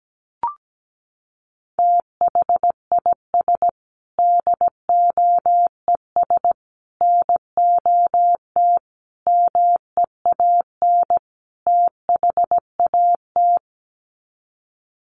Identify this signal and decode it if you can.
Morse code, heard as a keyed tone, about 17 wpm, 700 Hz. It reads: THIS DOES NOT MEAN THAT
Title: Sets of 5 Words Encore2 17wpm